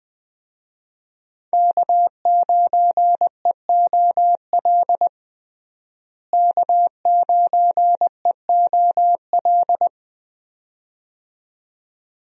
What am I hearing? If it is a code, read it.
Morse code, heard as a keyed tone, about 20 wpm, 700 Hz. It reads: K9EOL K9EOL
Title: USA Call Signs - 5 Characters - Repeated 2x - 20wpm